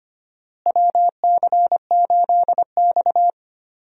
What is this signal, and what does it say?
Morse code, heard as a keyed tone, about 25 wpm, 700 Hz. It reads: WC8X
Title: USA Call Signs - 4 Characters - Repeated 3x - 25wpm